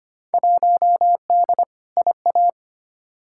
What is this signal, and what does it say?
Morse code, heard as a keyed tone, about 25 wpm, 700 Hz. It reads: 1D IA
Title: Field Day Exchanges - 25wpm